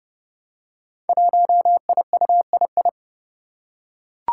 Morse code, heard as a keyed tone, about 30 wpm, 700 Hz. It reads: 1IUII
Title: Five Challenging Letter-Number Combinations 30wpm